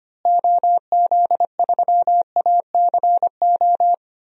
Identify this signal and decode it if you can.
Morse code, heard as a keyed tone, about 25 wpm, 700 Hz. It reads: OZ3ACO
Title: DXCC-100 Call Signs - 6 Characters - Repeated 3x - 25wpm